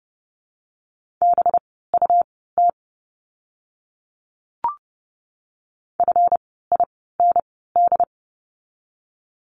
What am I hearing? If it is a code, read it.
Morse code, heard as a keyed tone, about 30 wpm, spaced as for 17 wpm, 700 Hz. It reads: BUT FIND
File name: course / 30:10wpm / 237 Lesson 38f - Letter q - Words 30wpm